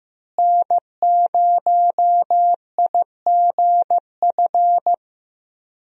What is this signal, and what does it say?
Morse code, heard as a keyed tone, about 15 wpm, 700 Hz. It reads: N0IGF
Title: USA Call Signs - 5 Characters - Repeated 3x - 15wpm